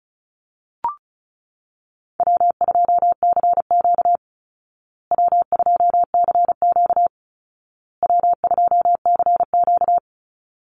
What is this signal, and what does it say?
Morse code, heard as a keyed tone, about 34 wpm, 700 Hz. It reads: W2CQ W2CQ W2CQ
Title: Easy Call Signs Repeated 3x - 35wpm